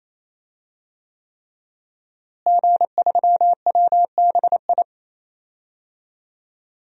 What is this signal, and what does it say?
Morse code, heard as a keyed tone, about 28 wpm, 700 Hz. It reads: G3WBI
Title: DXCC-100 Call Signs - 5 Characters - Repeated 3x - 28wpm